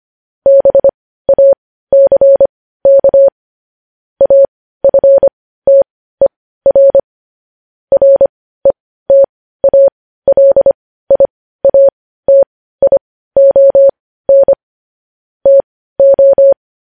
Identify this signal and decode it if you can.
Morse code, heard as a keyed tone, about 25 wpm, spaced as for 15 wpm, 550 Hz. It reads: BACK AFTER RETALIATION TO
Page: news headlines in morse code at 15 wpm